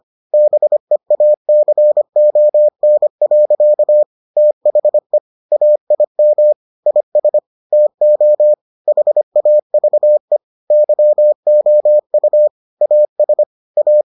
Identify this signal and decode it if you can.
Morse code, heard as a keyed tone, about 25 wpm, 600 Hz. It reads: BEACON. THE AIM IS TO HAVE YOU AS A